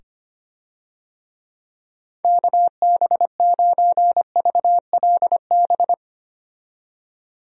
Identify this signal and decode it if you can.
Morse code, heard as a keyed tone, about 25 wpm, 700 Hz. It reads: KB9VLB
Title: USA Call Signs - 6 Characters - 25wpm